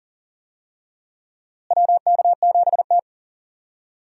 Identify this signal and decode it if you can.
Morse code, heard as a keyed tone, about 40 wpm, 700 Hz. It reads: WK7T
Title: USA Call Signs - 4 Characters - Repeated 2x - 40wpm